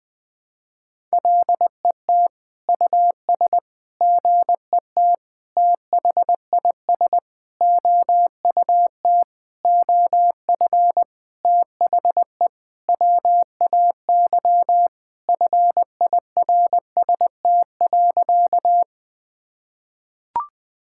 Morse code, heard as a keyed tone, about 21 wpm, 700 Hz. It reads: LET US GET THIS OUT OF THE WAY FIRST.
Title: Sentences with 7-Word Phrases - 20wpm